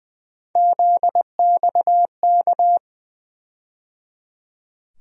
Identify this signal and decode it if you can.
Morse code, heard as a keyed tone, about 20 wpm, 700 Hz. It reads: ZXK
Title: USA Call Sign Postfixes - 3 Letters 20wpm